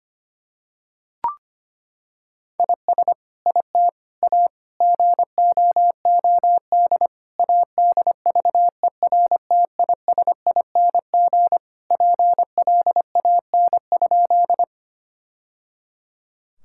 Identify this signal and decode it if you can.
Morse code, heard as a keyed tone, about 25 wpm, 700 Hz. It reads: IS IT A GOOD ADVERTISING PLAN?
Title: Sentences from Top 1000 Words - Review Part 6 - 25wpm